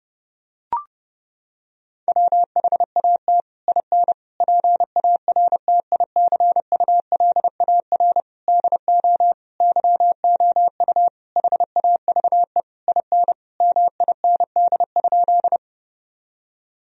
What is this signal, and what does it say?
Morse code, heard as a keyed tone, about 30 wpm, 700 Hz. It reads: WHAT IN PARTICULAR DO YOU HAVE IN MIND?
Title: Sentences from Top 1000 Words - Review Part 4 - 30wpm